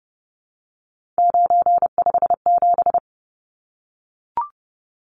Morse code, heard as a keyed tone, about 30 wpm, 700 Hz. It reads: 957
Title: Numbers - 3-Digit 30wpm